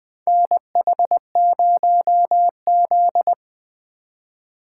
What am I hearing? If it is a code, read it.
Morse code, heard as a keyed tone, about 20 wpm, 700 Hz. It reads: NH0Z
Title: USA Call Signs - 4 Characters - 3x Character Spacing - 20wpm